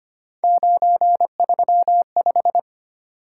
Morse code, heard as a keyed tone, about 25 wpm, 700 Hz. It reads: 935
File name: Three Letter-Number Combinations - 25wpm